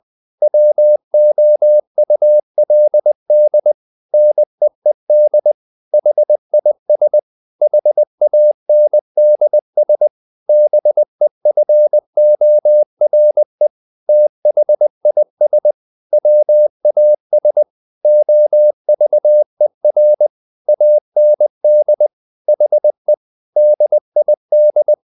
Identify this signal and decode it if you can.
Morse code, heard as a keyed tone, about 20 wpm, 600 Hz. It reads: WOULD NEED HIS HANDS BEFORE THIS WAS OVER AND HE DID
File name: the old man and the sea / the old man and the sea part 1 20wpm